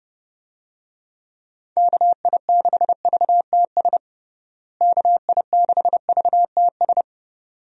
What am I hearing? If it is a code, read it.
Morse code, heard as a keyed tone, about 30 wpm, 700 Hz. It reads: KI6VTS KI6VTS
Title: USA Call Signs - 6 Characters - Repeated 2x - 30wpm